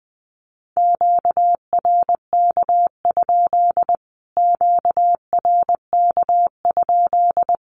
Morse code, heard as a keyed tone, about 20 wpm, 700 Hz. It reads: QRK? QRK?
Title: Q-Codes Repeated 3x 20wpm